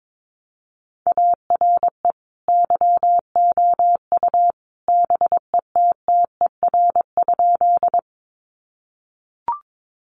Morse code, heard as a keyed tone, about 22 wpm, 700 Hz. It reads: ARE YOU BETTER?
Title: Sentences from Top 400 Words 22wpm